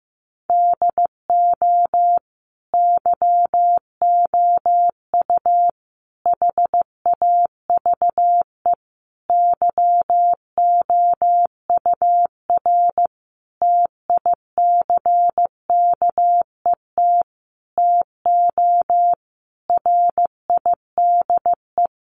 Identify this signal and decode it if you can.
Morse code, heard as a keyed tone, about 15 wpm, 700 Hz. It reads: DO YOU HAVE YOUR TICKET TO RIDE
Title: Sentences from Top 1400 Words 15wpm